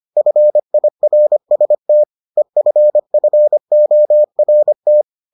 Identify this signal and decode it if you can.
Morse code, heard as a keyed tone, about 25 wpm, 600 Hz. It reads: FIRST EFFORT